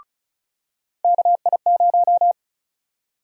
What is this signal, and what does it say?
Morse code, heard as a keyed tone, about 36 wpm, 700 Hz. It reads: KI0
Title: USA Call Sign Prefixes - 35wpm